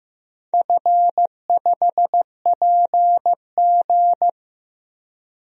Morse code, heard as a keyed tone, about 15 wpm, 700 Hz. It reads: F5PG
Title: DXCC-100 Call Signs - 4 Characters - Repeated 2x - 15wpm